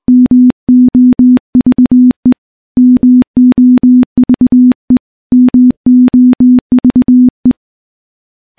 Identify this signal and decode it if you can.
Morse code, heard as a keyed tone, about 20 wpm, 250 Hz. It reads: MOVE MOVE MOVE
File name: morse / morse-training-20WPM